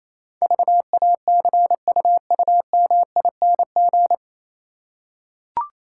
Morse code, heard as a keyed tone, about 27 wpm, 700 Hz. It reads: VACUUMING
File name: Easy Words - Speed-Racing with Hints - 22wpm